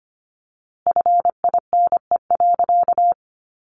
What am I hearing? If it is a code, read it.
Morse code, heard as a keyed tone, about 25 wpm, 700 Hz. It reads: FINE.
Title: A Sound Of Thunder 25wpm